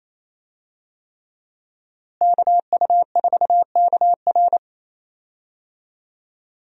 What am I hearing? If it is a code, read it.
Morse code, heard as a keyed tone, about 28 wpm, 700 Hz. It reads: KU4KR
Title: USA Call Signs - 5 Characters - Repeated 3x - 28wpm